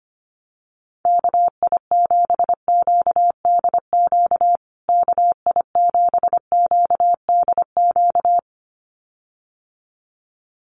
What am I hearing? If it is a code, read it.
Morse code, heard as a keyed tone, about 25 wpm, 700 Hz. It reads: KI7QDQ KI7QDQ
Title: USA Call Signs Repeated 2x 25wpm